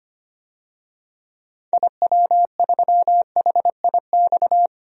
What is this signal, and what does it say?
Morse code, heard as a keyed tone, about 25 wpm, 700 Hz. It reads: IW3HIX
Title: DXCC-100 Call Signs - 6 Characters - 25wpm